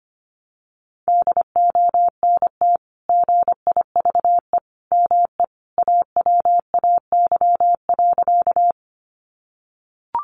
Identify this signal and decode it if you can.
Morse code, heard as a keyed tone, about 25 wpm, 700 Hz. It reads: DONT GIVE ME AWAY.
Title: Sentences from Top 600 Words 25wpm